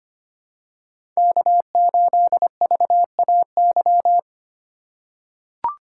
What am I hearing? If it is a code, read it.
Morse code, heard as a keyed tone, about 25 wpm, 700 Hz. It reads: K8VAY
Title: USA Call Signs - 5 Characters - 3x Character Spacing - 25wpm